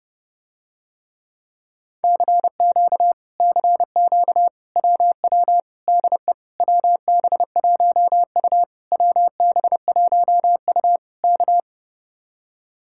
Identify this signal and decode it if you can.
Morse code, heard as a keyed tone, about 30 wpm, 700 Hz. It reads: CQ CQ WW DE WB1U WB1U K